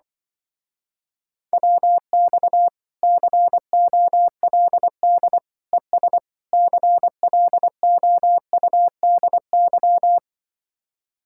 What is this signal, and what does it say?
Morse code, heard as a keyed tone, about 24 wpm, 700 Hz. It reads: WX COLD ES CLOUDY
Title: Speed-Racing Common QSO Phrases Encore - 20wpm